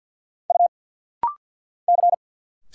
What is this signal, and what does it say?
Morse code, heard as a keyed tone, about 46 wpm, 700 Hz. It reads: U C
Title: Single Letters - Rapid-Fire 50wpm